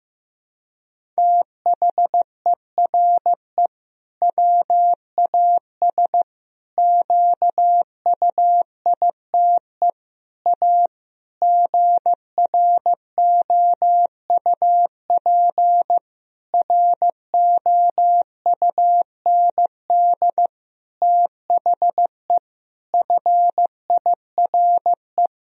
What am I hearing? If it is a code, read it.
Morse code, heard as a keyed tone, about 15 wpm, 700 Hz. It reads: THERE WAS QUITE A GROUP ROUND THE FIRE
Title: Sentences from Top 1000 Words - Review Part 1 - 15wpm